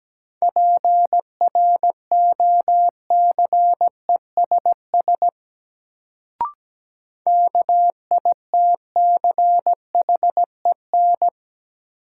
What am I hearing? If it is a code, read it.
Morse code, heard as a keyed tone, about 17 wpm, 700 Hz. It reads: PROCESS KITCHEN
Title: Sentences from Top 1400 Words - New Word Drill Alt - 17wpm